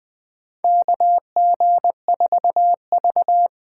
Morse code, heard as a keyed tone, about 20 wpm, 700 Hz. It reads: KG4V
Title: USA Call Signs - 6 Characters - 8x Character Spacing - 20wpm